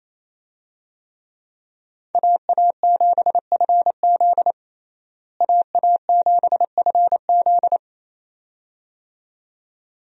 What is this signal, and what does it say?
Morse code, heard as a keyed tone, about 28 wpm, 700 Hz. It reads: AA7FZ AA7FZ
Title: USA Call Signs - 5 Characters - Repeated 2x - 28wpm